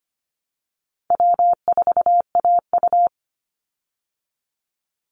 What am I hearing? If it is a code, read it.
Morse code, heard as a keyed tone, about 25 wpm, 700 Hz. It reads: W4AU
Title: Easy Call Signs Repeated 2x - 25wpm